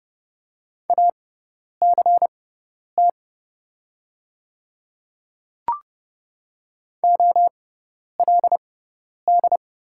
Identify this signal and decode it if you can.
Morse code, heard as a keyed tone, about 30 wpm, spaced as for 10 wpm, 700 Hz. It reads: ACT OLD
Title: 6x Character Spacing - Top 200 Words - 30wpm